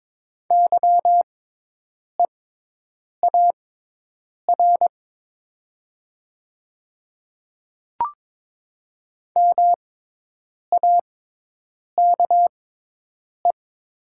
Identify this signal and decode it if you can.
Morse code, heard as a keyed tone, about 22 wpm, spaced as for 8 wpm, 700 Hz. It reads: YEAR MAKE
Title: 6x Character Spacing - Top 200 Words - 22wpm